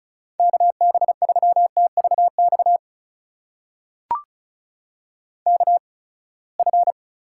Morse code, heard as a keyed tone, about 34 wpm, spaced as for 24 wpm, 700 Hz. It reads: KB3TVX KF
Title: USA Call Signs - 6 Characters - 8x Character Spacing - 35wpm